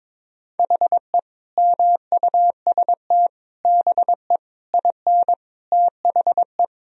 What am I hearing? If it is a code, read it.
Morse code, heard as a keyed tone, about 22 wpm, 700 Hz. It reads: HE MUST BE IN THE
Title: Sentences from Top 1000 Words - Review Part 2 - 22wpm